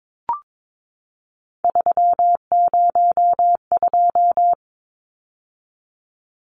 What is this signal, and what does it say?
Morse code, heard as a keyed tone, about 22 wpm, 700 Hz. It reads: 302
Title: Numbers - 3-Digit 22wpm